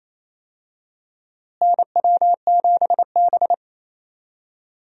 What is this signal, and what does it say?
Morse code, heard as a keyed tone, about 28 wpm, 700 Hz. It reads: NW7B
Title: USA Call Signs - 4 Characters - Repeated 2x - 28wpm